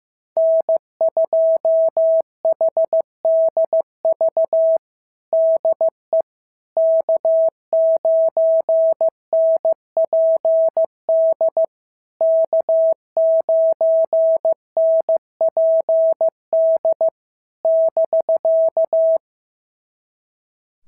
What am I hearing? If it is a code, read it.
Morse code, heard as a keyed tone, about 15 wpm, 650 Hz. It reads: N2HDV DE K9NPD K9NPD <BK>
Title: SOTA QSO Practice 15wpm